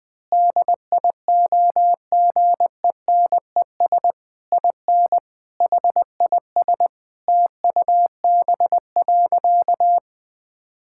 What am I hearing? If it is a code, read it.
Morse code, heard as a keyed tone, about 20 wpm, 700 Hz. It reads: DIOGENES IN HIS TUB.
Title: Pangram Challenge v2 - 20wpm